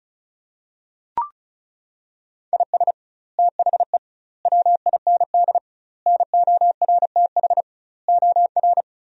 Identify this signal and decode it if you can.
Morse code, heard as a keyed tone, about 36 wpm, 700 Hz. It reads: IS THE WIND NORTH OR
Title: 2x Word-Spacing - Sentences from Top 400 Words - 35wpm